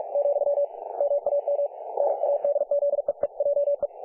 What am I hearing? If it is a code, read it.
Morse code, heard as a keyed tone, about 47 wpm, 570 Hz, with radio noise. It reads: , MAM NADZEEJE
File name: dlazaawansowanych / nagraniazpasma / websdr recording 3564.7kHz